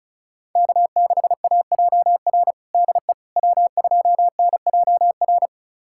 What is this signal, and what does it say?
Morse code, heard as a keyed tone, about 36 wpm, 700 Hz. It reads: K6AJR DE W2NJR
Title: USA Callsign DE Callsign - 5 Characters - 35wpm